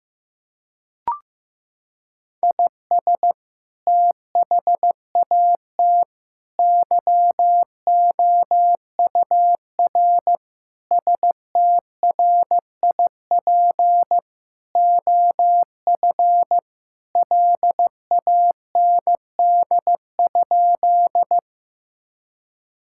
Morse code, heard as a keyed tone, about 15 wpm, 700 Hz. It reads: IS THAT YOUR STRIP OF LAND?
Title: Sentences from Top 1600 Words 15wpm